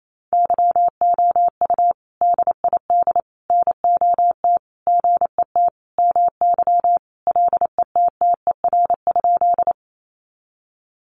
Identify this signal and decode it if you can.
Morse code, heard as a keyed tone, about 28 wpm, 700 Hz. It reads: YOU DID NOT GET MY LETTER?
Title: Sentences from Top 600 Words 28wpm